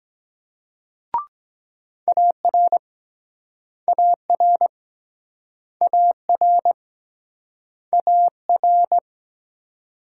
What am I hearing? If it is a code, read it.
Morse code, heard as a keyed tone, about 23 wpm, 700 Hz. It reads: AR AR AR AR